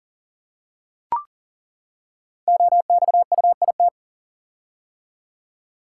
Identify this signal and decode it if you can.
Morse code, heard as a keyed tone, about 38 wpm, 690 Hz. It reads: OXUIT
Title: Code Groups - 5 Letters - 40wpm